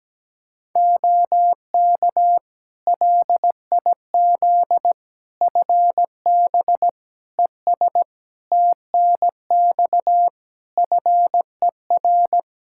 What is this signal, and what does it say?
Morse code, heard as a keyed tone, about 17 wpm, 700 Hz. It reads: OK LIZ FB ES TNX FER
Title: K1USN SST QSO Practice 17wpm